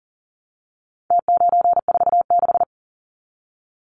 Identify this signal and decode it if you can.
Morse code, heard as a keyed tone, about 40 wpm, 700 Hz. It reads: T946
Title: Numbers - Contest Style - 4-Digit 40wpm